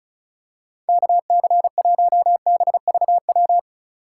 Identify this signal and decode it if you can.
Morse code, heard as a keyed tone, about 36 wpm, 700 Hz. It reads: KC1BVW
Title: USA Call Signs - 6 Characters - 4x Character Spacing - 35wpm